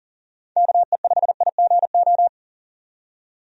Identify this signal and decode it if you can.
Morse code, heard as a keyed tone, about 41 wpm, 700 Hz. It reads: KE5IGO
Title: USA Call Signs - 6 Characters - 40wpm